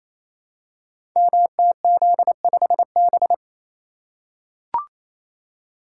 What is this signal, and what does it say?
Morse code, heard as a keyed tone, about 28 wpm, 700 Hz. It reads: MTZ5B
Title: Code Groups - 5 Letter:Numbers - 28wpm